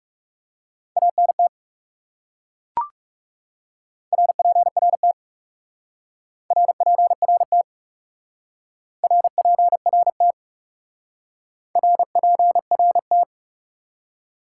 Morse code, heard as a keyed tone, about 36 wpm, 700 Hz. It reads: ANT RPRT RPRT RPRT RPRT
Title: Speed-Racing Top 90 QSO Elements 30wpm